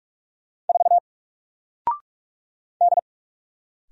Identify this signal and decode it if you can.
Morse code, heard as a keyed tone, about 44 wpm, 700 Hz. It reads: 4 D
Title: Single Character (Letter:Number:Punctuation:Prosign) Alt 45wpm